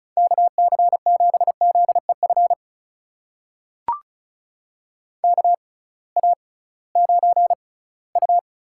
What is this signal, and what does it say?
Morse code, heard as a keyed tone, about 36 wpm, spaced as for 21 wpm, 700 Hz. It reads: KC7ZEF KA9U
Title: USA Call Signs - 6 Characters - 6x Character Spacing - 35wpm